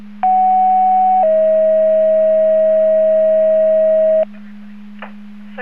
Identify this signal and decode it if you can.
Two-tone page, heard as a one-second tone, then a three-second tone, about 750 then 640 Hz. It is heard from 0.2 to 4.2 s.
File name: Portage 154.385 Paging